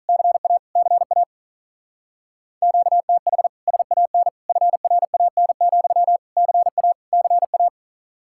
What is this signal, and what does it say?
Morse code, heard as a keyed tone, about 45 wpm, 700 Hz. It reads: CA CA QTH SAN FRAN, CA CA